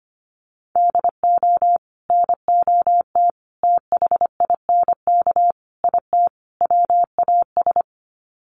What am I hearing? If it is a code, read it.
Morse code, heard as a keyed tone, about 25 wpm, 700 Hz. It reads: DO NOT THINK IT WAS
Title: Sets of 5 Words Encore 25wpm